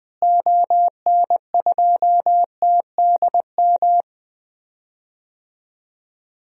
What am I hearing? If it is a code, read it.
Morse code, heard as a keyed tone, about 21 wpm, 700 Hz. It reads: ON2TDM
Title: DXCC-100 Call Signs - 6 Characters - Repeated 2x - 20wpm